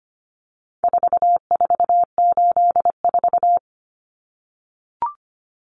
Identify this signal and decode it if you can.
Morse code, heard as a keyed tone, about 25 wpm, 700 Hz. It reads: 4484
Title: Common Cut Numbers - 4-Digit 25wpm